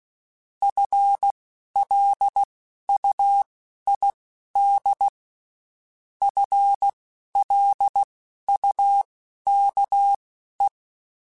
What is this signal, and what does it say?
Morse code, heard as a keyed tone, about 16 wpm, spaced as for 11 wpm, 800 Hz. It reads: FLUID FLUKE
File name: Spaced / FL Runs sp mp3s / FL Runs sp @15